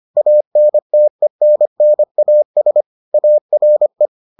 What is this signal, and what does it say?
Morse code, heard as a keyed tone, about 25 wpm, 600 Hz. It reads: ANTENNAS ARE